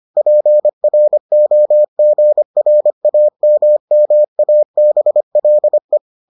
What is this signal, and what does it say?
Morse code, heard as a keyed tone, about 25 wpm, 600 Hz. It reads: PROGRAMMABLE